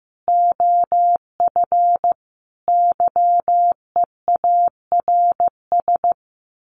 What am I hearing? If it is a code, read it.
Morse code, heard as a keyed tone, about 15 wpm, 700 Hz. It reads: OF YEARS
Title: Sets of 5 Words 15wpm